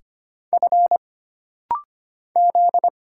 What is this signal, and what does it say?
Morse code, heard as a keyed tone, about 25 wpm, 700 Hz. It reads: F Z
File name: Single Character (Letter:Number:Punctuation:Prosign) - Rapid-Fire - 25wpm